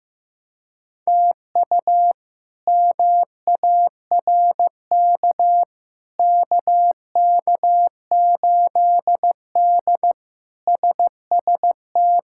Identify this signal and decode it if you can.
Morse code, heard as a keyed tone, about 16 wpm, 700 Hz. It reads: TU MARK KK8D SST
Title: K1USN SST QSO Practice - 15wpm